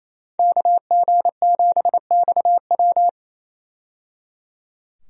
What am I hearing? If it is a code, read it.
Morse code, heard as a keyed tone, about 27 wpm, 700 Hz. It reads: KG7XW
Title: USA Call Signs - 5 Characters - 28wpm